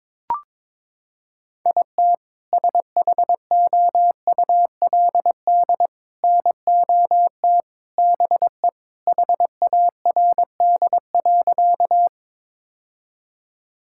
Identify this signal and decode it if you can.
Morse code, heard as a keyed tone, about 21 wpm, 700 Hz. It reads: IT SHOULD NOT BE HARD.
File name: practice / 22wpm / Sentences with 2-Word Phrases - 22wpm